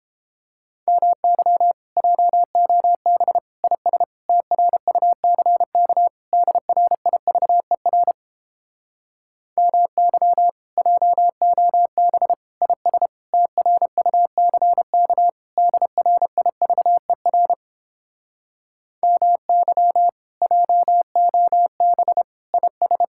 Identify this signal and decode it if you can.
Morse code, heard as a keyed tone, about 31 wpm, 700 Hz. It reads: MY JOB IS TRUCK DRIVER MY JOB IS TRUCK DRIVER MY JOB IS